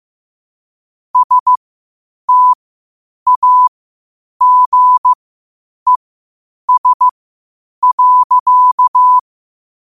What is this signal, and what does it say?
Morse code, heard as a keyed tone, about 15 wpm, spaced as for 8 wpm, 1000 Hz. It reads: STAGES.
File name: qotdfiles / 8wpm